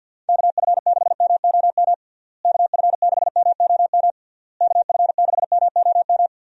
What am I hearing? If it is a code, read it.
Morse code, heard as a keyed tone, about 52 wpm, 700 Hz. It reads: KF6MOM KF6MOM KF6MOM
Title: USA Call Signs - 6 Characters - Repeated 3x - 50wpm